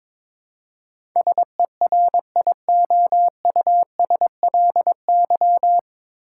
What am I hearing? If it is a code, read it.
Morse code, heard as a keyed tone, about 22 wpm, 700 Hz. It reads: SERIOUSLY
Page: Morse Code Ninja practice 22wpm